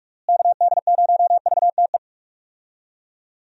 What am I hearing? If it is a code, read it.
Morse code, heard as a keyed tone, about 46 wpm, 700 Hz. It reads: KD0VTE
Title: USA Call Signs - 6 Characters - Repeated 3x - 45wpm